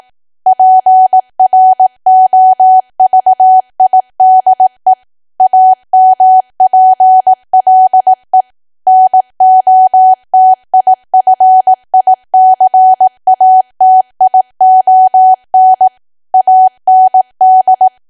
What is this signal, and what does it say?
Morse code, heard as a keyed tone, about 18 wpm, 750 Hz. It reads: PROVIDE AMPLE NOTIFICATION AND
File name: cwa / W1AW-Level2 / 18WPM